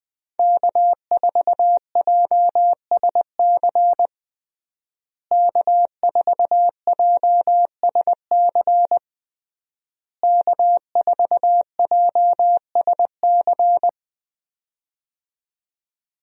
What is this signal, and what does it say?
Morse code, heard as a keyed tone, about 20 wpm, 700 Hz. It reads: K4JSC K4JSC K4JSC
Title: USA Call Signs - 5 Characters - Repeated 3x - 20wpm